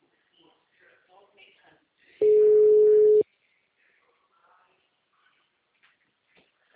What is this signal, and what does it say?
Ringback tone (Europe, 425 Hz), with speech.
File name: out- -2036- - - .9467